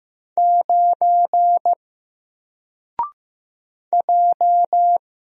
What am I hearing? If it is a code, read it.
Morse code, heard as a keyed tone, about 15 wpm, 700 Hz. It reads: 9 J
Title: Single Letter-Number 15wpm